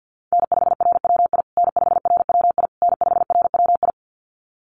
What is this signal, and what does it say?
Morse code, heard as a keyed tone, about 50 wpm, 700 Hz. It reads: N5RWI N5RWI N5RWI
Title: USA Call Signs Repeated 3x 50wpm